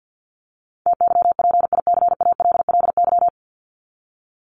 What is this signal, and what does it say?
Morse code, heard as a keyed tone, about 50 wpm, 700 Hz. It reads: TYPICALLY
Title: LY Suffix - Top 100 Words with Base Words 50wpm